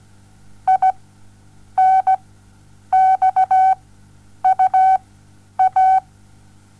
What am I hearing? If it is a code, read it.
Morse code, heard as a keyed tone, about 17 wpm, spaced as for 9 wpm, 760 Hz. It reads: INXUA
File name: Onlinelehrgang / morsen / DJ4UF Morsen07